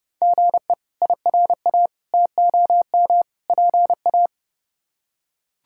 Morse code, heard as a keyed tone, about 30 wpm, 700 Hz. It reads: GE IRA TOM PA
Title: K1USN SST QSO Practice - 30wpm